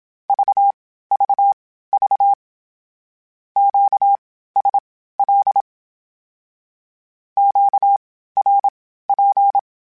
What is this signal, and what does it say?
Morse code, heard as a keyed tone, about 27 wpm, spaced as for 15 wpm, 800 Hz. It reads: VVV QSL QRP
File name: (spaced) / Q-Calls Spaced-mp3s / Q Calls sp Once @25